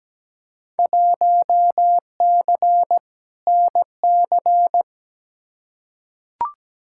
Morse code, heard as a keyed tone, about 17 wpm, 700 Hz. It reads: 1C NC
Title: Field Day Exchanges - 17wpm